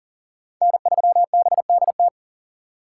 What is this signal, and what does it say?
Morse code, heard as a keyed tone, about 38 wpm, 700 Hz. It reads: N3BDT
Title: USA Call Signs - 3x Character Spacing - 40wpm